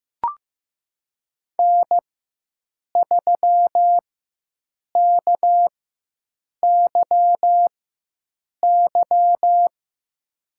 Morse code, heard as a keyed tone, about 15 wpm, spaced as for 7 wpm, 700 Hz. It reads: N3KYY
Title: USA Call Signs - 5 Characters - 4x Character Spacing - 15wpm